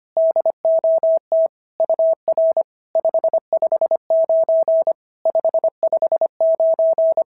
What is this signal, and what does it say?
Morse code, heard as a keyed tone, about 25 wpm, 650 Hz. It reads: DOT UR 559 559
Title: SOTA QSO Practice 25wpm